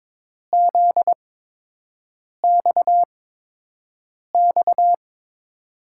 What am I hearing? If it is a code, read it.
Morse code, heard as a keyed tone, about 21 wpm, spaced as for 6 wpm, 700 Hz. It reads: ZXX
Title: USA Call Signs - 5 Characters - 8x Character Spacing - 22wpm